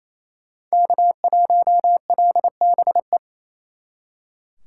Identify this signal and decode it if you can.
Morse code, heard as a keyed tone, about 36 wpm, 700 Hz. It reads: K1LBE